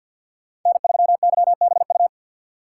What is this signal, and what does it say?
Morse code, heard as a keyed tone, about 46 wpm, 700 Hz. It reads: N3YBU
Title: USA Call Signs - 5 Characters - 3x Character Spacing - 50wpm